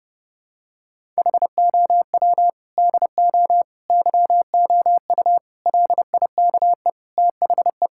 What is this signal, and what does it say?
Morse code, heard as a keyed tone, about 30 wpm, 700 Hz. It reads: HOW DO YOU LIKE THE
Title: Sentences from Top 1000 Words - Review Part 6 - 30wpm